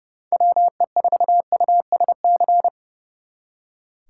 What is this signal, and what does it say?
Morse code, heard as a keyed tone, about 36 wpm, 700 Hz. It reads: WE4USC
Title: Speed Racing - USA 2x3 Call Signs 30wpm